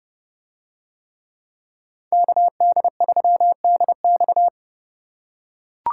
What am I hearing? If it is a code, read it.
Morse code, heard as a keyed tone, about 29 wpm, 700 Hz. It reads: KD3DX